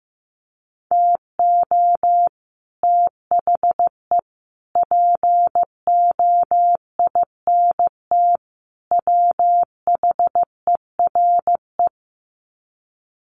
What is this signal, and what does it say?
Morse code, heard as a keyed tone, about 15 wpm, 700 Hz. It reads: TO THE POINT WHERE
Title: Sets of 4 Words 15wpm